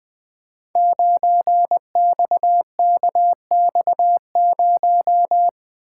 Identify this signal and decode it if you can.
Morse code, heard as a keyed tone, about 20 wpm, 700 Hz. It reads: 9XKX0
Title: Code Groups - 5 Letter:Numbers - 20wpm